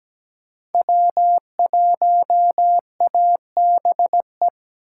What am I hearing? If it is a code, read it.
Morse code, heard as a keyed tone, about 17 wpm, 700 Hz. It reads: W1ABE
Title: USA Call Signs - 5 Characters - 4x Character Spacing - 17wpm